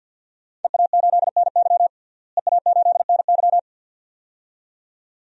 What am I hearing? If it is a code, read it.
Morse code, heard as a keyed tone, about 53 wpm, 700 Hz. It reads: EA8NY EA8NY
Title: DXCC-100 Call Signs - 5 Characters - Repeated 2x - 50wpm